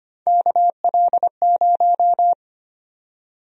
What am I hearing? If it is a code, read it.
Morse code, heard as a keyed tone, about 25 wpm, 700 Hz. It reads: KL0
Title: USA Call Sign Prefixes - 25wpm